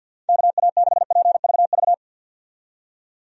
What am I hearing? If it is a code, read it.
Morse code, heard as a keyed tone, about 49 wpm, 700 Hz. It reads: KA6PVV
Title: USA Call Signs - 6 Characters - 50wpm